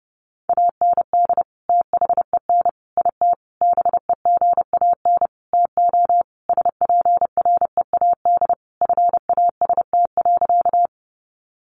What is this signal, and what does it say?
Morse code, heard as a keyed tone, about 30 wpm, 700 Hz. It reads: AND THEN IT BEGAN TO SPREAD FAST.
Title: Sentences from Top 1200 Words 30wpm